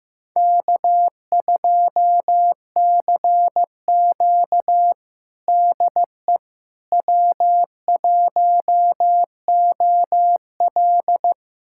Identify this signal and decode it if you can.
Morse code, heard as a keyed tone, about 15 wpm, 700 Hz. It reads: K2CQ DE W1OL
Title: USA Callsign DE Callsign - 4 Characters - 15wpm